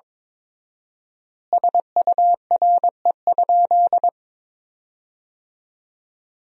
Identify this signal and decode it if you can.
Morse code, heard as a keyed tone, about 22 wpm, 700 Hz. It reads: SURE?
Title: 4x Word-Spacing - Sentences from Top 300 Words - 22wpm